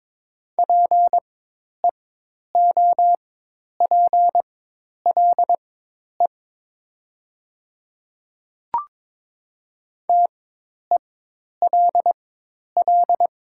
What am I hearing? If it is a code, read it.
Morse code, heard as a keyed tone, about 22 wpm, spaced as for 10 wpm, 700 Hz. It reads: PEOPLE TELL
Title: 4x Character Spacing - Top 300 Words - 22wpm